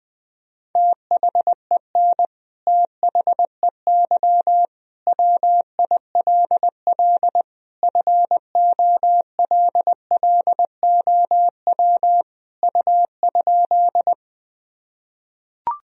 Morse code, heard as a keyed tone, about 20 wpm, 700 Hz. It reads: THEN THEY WILL FOLLOW U?
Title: Sentences from Top 200 Words and Abbreviations 20wpm